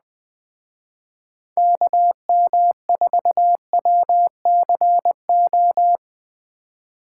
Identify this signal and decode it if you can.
Morse code, heard as a keyed tone, about 21 wpm, 700 Hz. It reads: KM4WCO